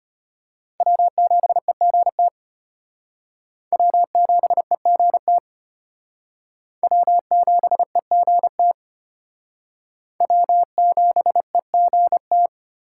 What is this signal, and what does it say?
Morse code, heard as a keyed tone, about 32 wpm, 700 Hz. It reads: W7EGT W7EGT W7EGT W7EGT